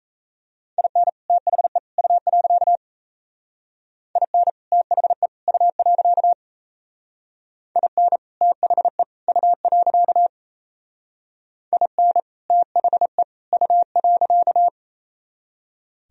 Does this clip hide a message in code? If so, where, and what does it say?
Yes: Morse code from 0.8 to 14.7 s, heard as a keyed tone, about 37 wpm, 700 Hz. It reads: IN THE U. IN THE U. IN THE U. IN THE U.